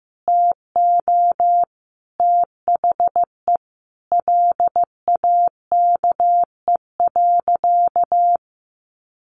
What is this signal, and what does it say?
Morse code, heard as a keyed tone, about 15 wpm, 700 Hz. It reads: TO THE LAKE.
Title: Sentences from Top 1500 Words 15wpm